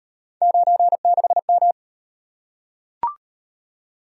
Morse code, heard as a keyed tone, about 34 wpm, 700 Hz. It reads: 96M